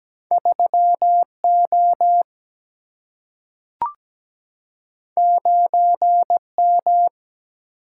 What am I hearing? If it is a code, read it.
Morse code, heard as a keyed tone, about 17 wpm, 700 Hz. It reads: 3 O 9 M
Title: Single-Letter Single-Number 17wpm